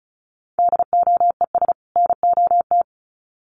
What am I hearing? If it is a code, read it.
Morse code, heard as a keyed tone, about 34 wpm, 700 Hz. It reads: DOES NOT
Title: Sets of 2 Words 35wpm